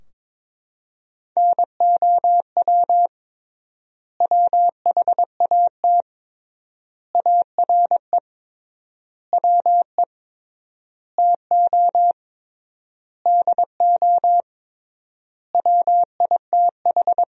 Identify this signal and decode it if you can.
Morse code, heard as a keyed tone, about 22 wpm, 700 Hz. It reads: NOW WHAT ARE WE TO DO WITH
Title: 3x Word-Spacing - Sentences from Top 100 Words - 22wpm